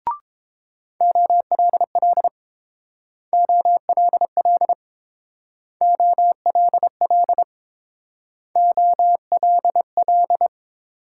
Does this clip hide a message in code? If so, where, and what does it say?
Yes: Morse code from 1.0 to 10.5 s, heard as a keyed tone, about 30 wpm, 700 Hz. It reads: OLL OLL OLL OLL